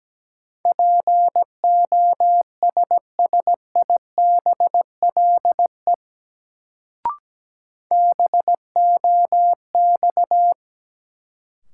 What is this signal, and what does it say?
Morse code, heard as a keyed tone, about 17 wpm, 700 Hz. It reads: POSSIBLE BOX
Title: Sentences from Top 500 Words - New Word Drill - 17wpm